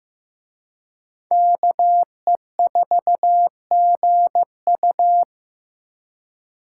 Morse code, heard as a keyed tone, about 18 wpm, 700 Hz. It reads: KE4GU
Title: Speed Racing - USA Call Signs 15wpm